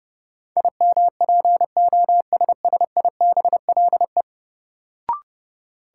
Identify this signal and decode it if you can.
Morse code, heard as a keyed tone, about 30 wpm, 700 Hz. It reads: IMPOSSIBLE
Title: Sentences from Top 1100 Words - New Word Drill - 30wpm